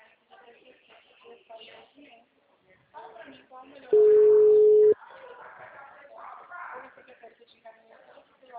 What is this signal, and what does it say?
Ringback tone (Europe, 425 Hz), with speech.